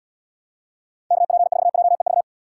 Morse code, heard as a keyed tone, about 72 wpm, 700 Hz. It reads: KP4JEV